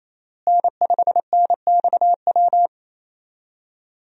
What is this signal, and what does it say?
Morse code, heard as a keyed tone, about 28 wpm, 700 Hz. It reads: N5NXW
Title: USA Call Signs - 5 Characters - Repeated 2x - 28wpm